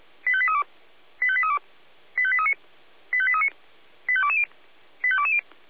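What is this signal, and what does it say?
Five-tone selective call (ZVEI), heard as quick runs of five tones, tones of about 70 ms. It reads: 85832 85832 85839 85839 85220 85220
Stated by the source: ZVEI five-tone selective call received on an SDR.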